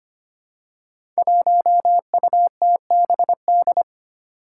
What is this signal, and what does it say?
Morse code, heard as a keyed tone, about 25 wpm, 700 Hz. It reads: 1UTBD
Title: Five Challenging Letter-Number Combinations 25wpm